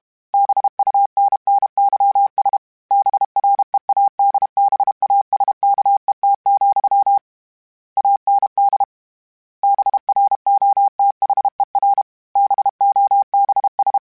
Morse code, heard as a keyed tone, about 32 wpm, 800 Hz. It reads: BUNNYS BREADBASKET, AND BROTHER BOBS